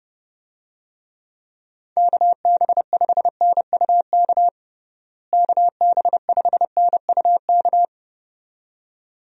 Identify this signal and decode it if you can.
Morse code, heard as a keyed tone, about 32 wpm, 700 Hz. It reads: KB5NUK KB5NUK
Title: USA Call Signs - 6 Characters - Repeated 2x - 30wpm